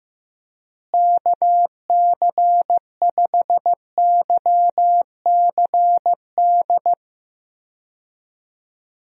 Morse code, heard as a keyed tone, about 16 wpm, 700 Hz. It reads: KC5YCD
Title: USA Call Signs - 6 Characters - Repeated 2x - 15wpm